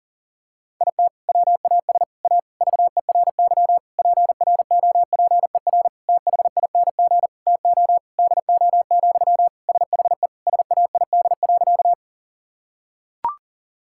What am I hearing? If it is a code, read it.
Morse code, heard as a keyed tone, about 37 wpm, 700 Hz. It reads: IT WAS A VERY PROPER THING TO DO, SHE SAID.
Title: Sentences from Top 1000 Words - Review Part 5 - 40wpm